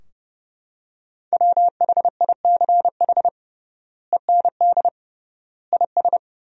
Morse code, heard as a keyed tone, about 30 wpm, 700 Hz. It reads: WHICH END IS
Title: 3x Word-Spacing - Sentences from Top 300 Words - 30wpm